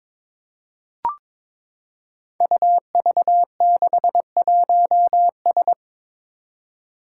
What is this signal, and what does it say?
Morse code, heard as a keyed tone, about 22 wpm, 700 Hz. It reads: UV61S
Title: Five Challenging Letter-Number Combinations 22wpm